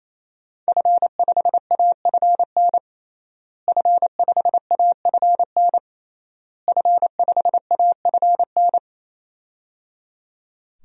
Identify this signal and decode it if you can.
Morse code, heard as a keyed tone, about 28 wpm, 700 Hz. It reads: F5AFN F5AFN F5AFN
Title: DXCC-100 Call Signs - 5 Characters - Repeated 3x - 28wpm